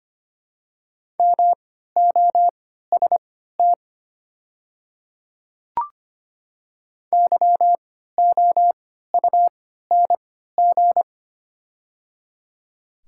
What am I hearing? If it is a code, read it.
Morse code, heard as a keyed tone, about 25 wpm, spaced as for 14 wpm, 690 Hz. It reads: MOST YOUNG
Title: 3x Character Spacing - Top 500 Words - 25wpm